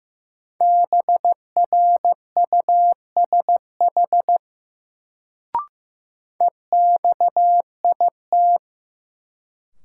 Morse code, heard as a keyed tone, about 15 wpm, 700 Hz. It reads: BRUSH EXIT
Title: Sentences from Top 1100 Words - New Word Drill - 15wpm